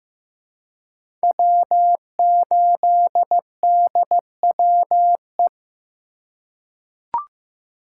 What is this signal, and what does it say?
Morse code, heard as a keyed tone, about 15 wpm, 700 Hz. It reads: W8DWE
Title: USA Call Signs - 5 Characters - 3x Character Spacing - 15wpm